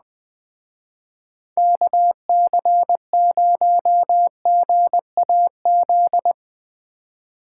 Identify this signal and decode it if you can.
Morse code, heard as a keyed tone, about 20 wpm, 700 Hz. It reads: KC0GAZ
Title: Speed Racing - USA 2x3 Call Signs 17wpm